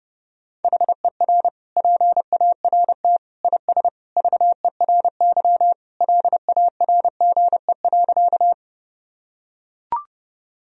Morse code, heard as a keyed tone, about 30 wpm, 700 Hz. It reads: HER PART IS VERY LARGE.
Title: Sentences from Top 1000 Words - Review Part 1 - 30wpm